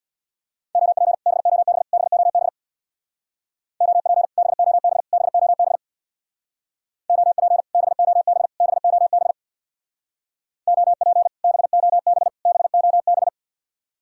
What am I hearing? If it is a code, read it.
Morse code, heard as a keyed tone, about 66 wpm, 700 Hz. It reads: OP BOB BOB OP BOB BOB OP BOB BOB OP BOB BOB